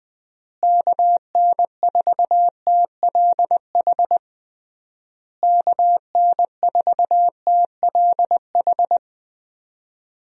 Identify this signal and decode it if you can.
Morse code, heard as a keyed tone, about 20 wpm, 700 Hz. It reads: KN4TLH KN4TLH
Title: USA Call Signs - 6 Characters - Repeated 3x - 20wpm